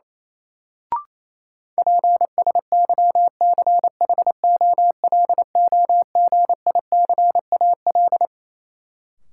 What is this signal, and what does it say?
Morse code, heard as a keyed tone, about 28 wpm, 700 Hz. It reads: PSYCHOLOGICAL
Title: Sentences from Top 1400 Words - New Word Drill - 28wpm